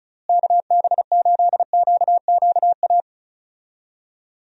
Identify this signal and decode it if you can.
Morse code, heard as a keyed tone, about 36 wpm, 700 Hz. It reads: KB8QQA
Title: USA Call Signs - 6 Characters - Repeated 2x - 35wpm